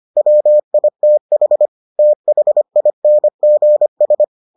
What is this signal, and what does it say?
Morse code, heard as a keyed tone, about 25 wpm, 600 Hz. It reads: WITH THINGS